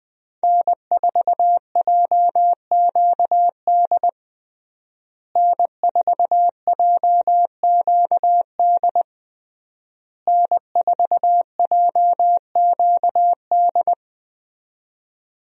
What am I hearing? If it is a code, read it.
Morse code, heard as a keyed tone, about 21 wpm, 700 Hz. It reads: N4JQD N4JQD N4JQD
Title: USA Call Signs - 5 Characters - Repeated 3x - 20wpm